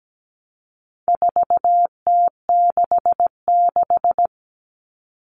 Morse code, heard as a keyed tone, about 17 wpm, 700 Hz. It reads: 4T66
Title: All Cut Numbers - 4-Digit 17wpm